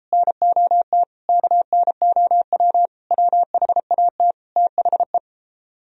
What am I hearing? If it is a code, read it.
Morse code, heard as a keyed tone, about 37 wpm, 700 Hz. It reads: NOT KNOW WHAT THE